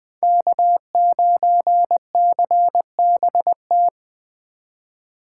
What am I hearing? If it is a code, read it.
Morse code, heard as a keyed tone, about 20 wpm, 700 Hz. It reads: K9CBT
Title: Speed Racing - USA 1x3 Call Signs 17wpm